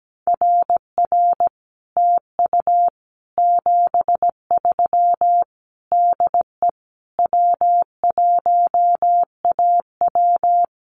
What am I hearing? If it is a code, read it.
Morse code, heard as a keyed tone, about 17 wpm, 700 Hz. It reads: RR TU 73 DE W1AW
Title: Common QSO Phrases Encore 17wpm